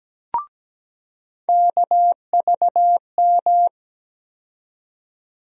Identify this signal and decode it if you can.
Morse code, heard as a keyed tone, about 17 wpm, 700 Hz. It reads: KVM
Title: USA Call Sign Postfixes - 3 Letters 17wpm